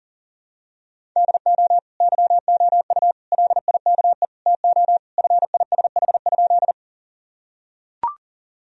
Morse code, heard as a keyed tone, about 39 wpm, 700 Hz. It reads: DO YOU LIKE TO FISH?
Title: Sentences from Top 1000 Words - Review Part 1 - 40wpm